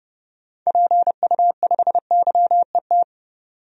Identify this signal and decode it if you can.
Morse code, heard as a keyed tone, about 30 wpm, 700 Hz. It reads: PU5YET
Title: DXCC-100 Call Signs - 6 Characters - 30wpm